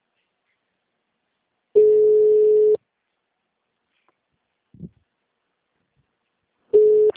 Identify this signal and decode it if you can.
Ringback tone (Europe, 425 Hz).